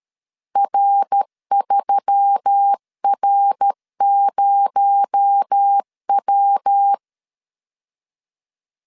Morse code, heard as a keyed tone, about 13 wpm, 800 Hz. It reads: R3R0W
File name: CW / DET / 1 6 1500C S B 12wpm 3431